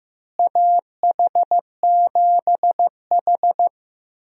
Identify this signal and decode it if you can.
Morse code, heard as a keyed tone, about 15 wpm, 700 Hz. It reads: AH7H
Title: USA Call Signs - 4 Characters - Repeated 3x - 15wpm